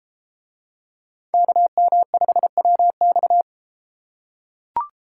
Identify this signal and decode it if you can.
Morse code, heard as a keyed tone, about 33 wpm, 700 Hz. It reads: KM5WX